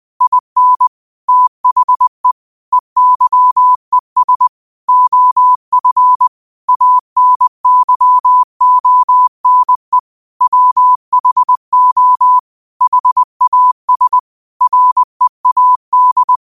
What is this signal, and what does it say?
Morse code, heard as a keyed tone, about 20 wpm, 1000 Hz. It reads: IN THE EYES OF ANYONE WHO HAS READ